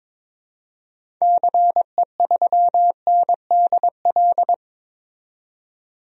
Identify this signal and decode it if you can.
Morse code, heard as a keyed tone, about 22 wpm, 700 Hz. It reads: CE3NDL
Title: DXCC-100 Call Signs - 6 Characters - 22wpm